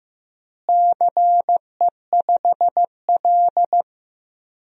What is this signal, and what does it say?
Morse code, heard as a keyed tone, about 15 wpm, 700 Hz. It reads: CE5L
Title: DXCC-100 Call Signs - 4 Characters - Repeated 3x - 15wpm